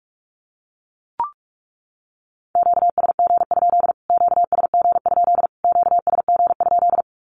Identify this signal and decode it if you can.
Morse code, heard as a keyed tone, about 44 wpm, 700 Hz. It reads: QSG? QSG? QSG?
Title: Q-Codes Repeated 3x 45wpm